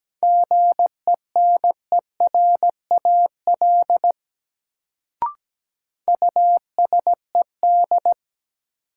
Morse code, heard as a keyed tone, about 17 wpm, 700 Hz. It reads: GENERAL USED
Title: Sentences from Top 600 Words - New Word Drill - 17wpm